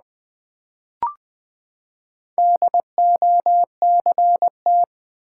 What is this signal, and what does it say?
Morse code, heard as a keyed tone, about 20 wpm, 700 Hz. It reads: DOCT
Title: Build A Word - 4 Letters At A Time - 8-Letter Words - 20wpm